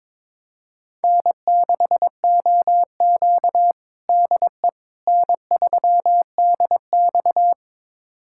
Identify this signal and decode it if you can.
Morse code, heard as a keyed tone, about 21 wpm, 700 Hz. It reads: N6OQ DE N3DX
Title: USA Callsign DE Callsign - 4 Characters - 22wpm